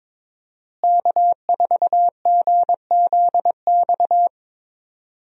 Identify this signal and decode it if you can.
Morse code, heard as a keyed tone, about 21 wpm, 700 Hz. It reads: K4GZX
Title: USA Call Signs - 5 Characters - Repeated 2x - 22wpm